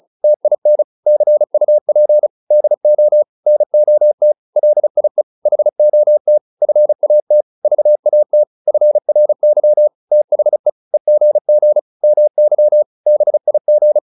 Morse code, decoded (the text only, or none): TIN CUP DO NOT LIE HOT FAT VAT FRY THE EGG MY BIG